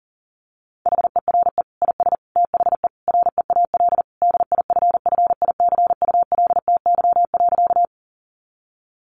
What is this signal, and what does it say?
Morse code, heard as a keyed tone, about 40 wpm, 700 Hz. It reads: HERE IS THE REAL DIFFICULTY.
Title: Sentences from Top 1500 Words 40wpm